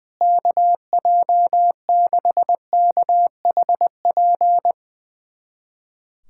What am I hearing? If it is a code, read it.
Morse code, heard as a keyed tone, about 21 wpm, 700 Hz. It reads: KJ6KHP
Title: USA Callsign DE Callsign - 6 Characters - 20wpm